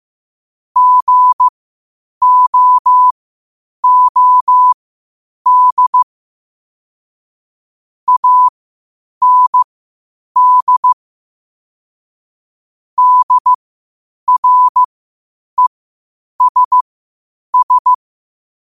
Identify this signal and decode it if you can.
Morse code, heard as a keyed tone, about 15 wpm, spaced as for 8 wpm, 1000 Hz. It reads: GOOD AND DRESS
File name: qotdfiles / 8wpm